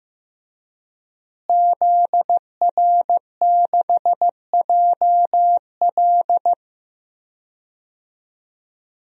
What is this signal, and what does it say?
Morse code, heard as a keyed tone, about 15 wpm, 700 Hz. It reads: ZR6JL
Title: DXCC-100 Call Signs - 5 Characters - Repeated 2x - 15wpm